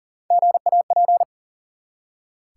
Morse code, heard as a keyed tone, about 43 wpm, 700 Hz. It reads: GAP
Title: Three Letter Words - 40wpm